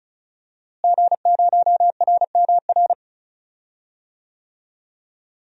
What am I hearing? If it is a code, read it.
Morse code, heard as a keyed tone, about 34 wpm, 700 Hz. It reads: G0RMR
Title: DXCC-100 Call Signs - 5 Characters - Repeated 3x - 35wpm